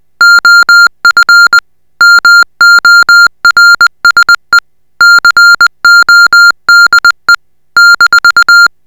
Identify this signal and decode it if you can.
Morse code, heard as a keyed tone, about 20 wpm, 1400 Hz. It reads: OF MORSE CODE -